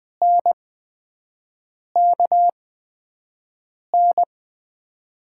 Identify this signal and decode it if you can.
Morse code, heard as a keyed tone, about 20 wpm, spaced as for 5 wpm, 700 Hz. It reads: NKN
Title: USA Call Signs - 8x Character Spacing - 20wpm